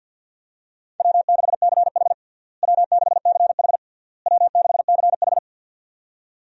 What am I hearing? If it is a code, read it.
Morse code, heard as a keyed tone, about 50 wpm, 700 Hz. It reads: W6CH W6CH W6CH
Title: USA Call Signs - 4 Characters - Repeated 3x - 50wpm